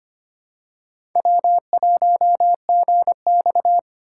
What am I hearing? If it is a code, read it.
Morse code, heard as a keyed tone, about 25 wpm, 700 Hz. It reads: W1GX
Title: USA Call Signs - 4 Characters - 2x Character Spacing - 25wpm